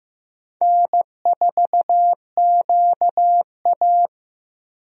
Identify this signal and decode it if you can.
Morse code, heard as a keyed tone, about 15 wpm, 700 Hz. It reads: N4QA
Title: USA Call Signs - 4 Characters - Repeated 3x - 15wpm